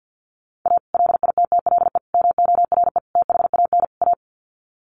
Morse code, heard as a keyed tone, about 48 wpm, 700 Hz. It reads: A LITTLE MORE THAN A
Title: Sets of 5 Words Encore2 50wpm